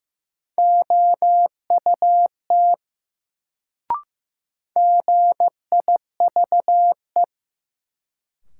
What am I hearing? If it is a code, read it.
Morse code, heard as a keyed tone, about 15 wpm, 700 Hz. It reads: OUT GIVE
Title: Sets of 2 Words - New Word Drill - 15wpm